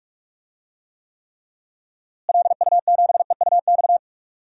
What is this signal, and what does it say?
Morse code, heard as a keyed tone, about 46 wpm, 700 Hz. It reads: LU7EUX
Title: DXCC-100 Call Signs - 6 Characters - Repeated 3x - 45wpm